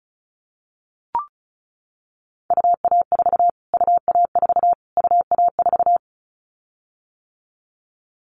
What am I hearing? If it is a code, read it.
Morse code, heard as a keyed tone, about 34 wpm, 700 Hz. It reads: UA4 UA4 UA4
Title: Callsign Prefixes for Top 25 DXCC - Repeated 3x - 35wpm